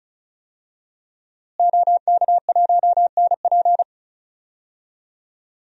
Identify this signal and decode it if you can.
Morse code, heard as a keyed tone, about 36 wpm, 700 Hz. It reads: OK1NP
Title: DXCC-100 Call Signs - 5 Characters - Repeated 3x - 35wpm